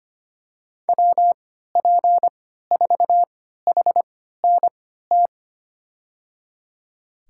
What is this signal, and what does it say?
Morse code, heard as a keyed tone, about 25 wpm, spaced as for 14 wpm, 700 Hz. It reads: WP4HNT
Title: USA Call Signs - 6 Characters - 3x Character Spacing - 25wpm